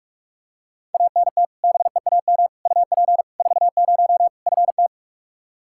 Morse code, heard as a keyed tone, about 47 wpm, 700 Hz. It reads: ANT BEAM UP 40 FT
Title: Practice QSO - Exchange 2 - Info - 45wpm